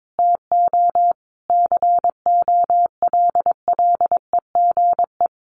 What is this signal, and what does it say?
Morse code, heard as a keyed tone, about 22 wpm, 700 Hz. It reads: TO COLLEGE
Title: Sentences from Top 1300 Words 22wpm